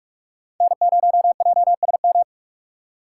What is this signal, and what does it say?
Morse code, heard as a keyed tone, about 45 wpm, 700 Hz. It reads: N0JSM